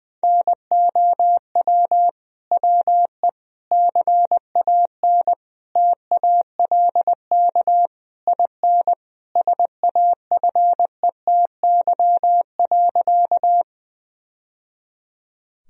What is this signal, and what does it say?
Morse code, heard as a keyed tone, about 21 wpm, 700 Hz. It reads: NOW WE CAN TALK IN SAFETY.
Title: Sentences from Top 1000 Words - Review Part 4 - 20wpm